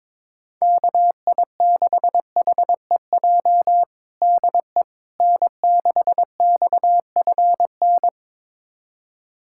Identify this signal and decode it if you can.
Morse code, heard as a keyed tone, about 22 wpm, 700 Hz. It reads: KI6HEJ DE N6XFN
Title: USA Callsign DE Callsign 22wpm